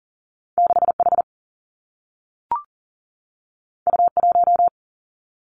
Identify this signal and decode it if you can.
Morse code, heard as a keyed tone, about 39 wpm, 700 Hz. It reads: 6 H U 1
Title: Two Challenging Letter-Number Combinations 40wpm